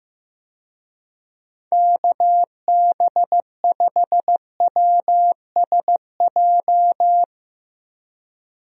Morse code, heard as a keyed tone, about 16 wpm, 700 Hz. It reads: KB5WSJ
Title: USA Call Signs - 6 Characters - Repeated 3x - 15wpm